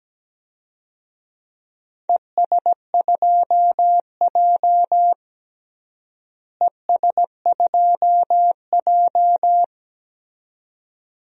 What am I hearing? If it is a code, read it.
Morse code, heard as a keyed tone, about 17 wpm, 700 Hz. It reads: ES2J ES2J
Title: DXCC-100 Call Signs - 4 Characters - Repeated 2x - 17wpm